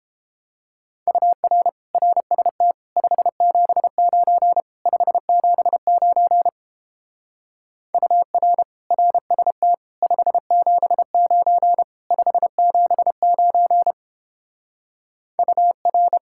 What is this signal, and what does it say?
Morse code, heard as a keyed tone, about 31 wpm, 700 Hz. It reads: UR RST 579 579 UR RST 579 579 UR